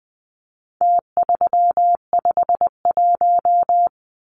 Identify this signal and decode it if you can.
Morse code, heard as a keyed tone, about 20 wpm, 700 Hz. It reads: T351
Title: Numbers - Contest Style - 4-Digit 20wpm